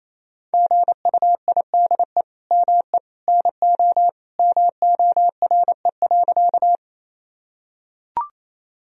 Morse code, heard as a keyed tone, about 28 wpm, 700 Hz. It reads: GUIDE ME NO MORE.
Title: Sentences from Top 1000 Words - Review Part 3 - 28wpm